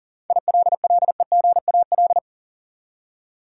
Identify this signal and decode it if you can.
Morse code, heard as a keyed tone, about 40 wpm, 700 Hz. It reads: ILLEGAL
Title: Seven Letter Words - 40wpm